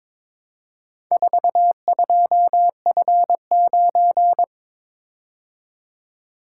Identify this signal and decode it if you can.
Morse code, heard as a keyed tone, about 22 wpm, 700 Hz. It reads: 42F9
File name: practice / 22wpm / Four Letter-Number Combinations - 22wpm